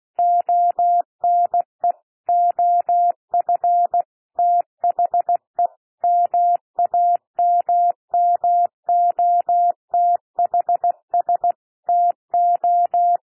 Morse code, heard as a keyed tone, about 16 wpm, 700 Hz. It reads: ONE OF THE MAMMOTHS TO
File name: cwmp3 / P1 SOUND PASSAGE 16 wpm